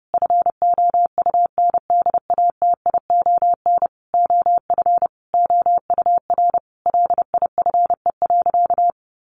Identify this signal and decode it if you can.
Morse code, heard as a keyed tone, about 30 wpm, 700 Hz. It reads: FOUNDATION OF OUR LIFE.
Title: Sentences from Top 1200 Words 30wpm